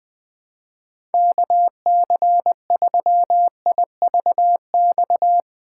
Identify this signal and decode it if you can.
Morse code, heard as a keyed tone, about 20 wpm, 700 Hz. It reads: KC3IVX
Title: USA Call Signs - 6 Characters - 8x Character Spacing - 20wpm